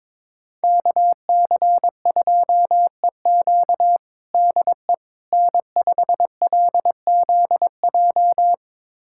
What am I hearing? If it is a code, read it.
Morse code, heard as a keyed tone, about 22 wpm, 700 Hz. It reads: KC2EQ DE N5LZJ
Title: USA Callsign DE Callsign - 5 Characters - 22wpm